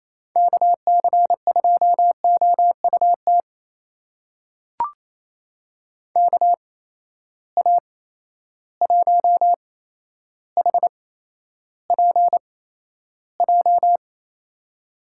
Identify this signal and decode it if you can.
Morse code, heard as a keyed tone, about 28 wpm, spaced as for 12 wpm, 700 Hz. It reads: KC2OUT KA1HPJ
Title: USA Call Signs - 6 Characters - 8x Character Spacing - 28wpm